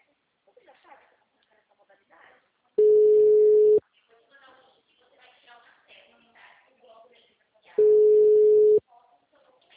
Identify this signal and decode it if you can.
Ringback tone (Europe, 425 Hz), with speech.